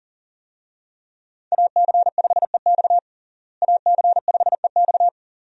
Morse code, heard as a keyed tone, about 37 wpm, 700 Hz. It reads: AC5EX AC5EX
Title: USA Call Signs - 5 Characters - Repeated 2x - 40wpm